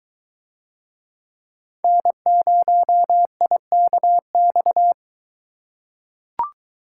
Morse code, heard as a keyed tone, about 23 wpm, 700 Hz. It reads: N0IKX